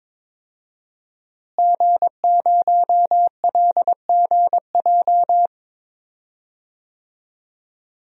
Morse code, heard as a keyed tone, about 22 wpm, 700 Hz. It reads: G0LGJ
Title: DXCC-100 Call Signs - 5 Characters - Repeated 2x - 22wpm